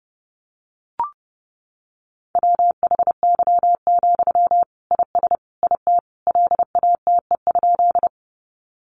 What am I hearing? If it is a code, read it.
Morse code, heard as a keyed tone, about 30 wpm, 700 Hz. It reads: WHY, IS IT LATE?
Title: Sentences from Top 300 Words 30wpm